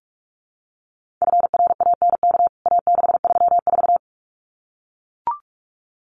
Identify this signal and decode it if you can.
Morse code, heard as a keyed tone, about 44 wpm, 700 Hz. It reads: FRANK A634
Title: CWT Exchange 45wpm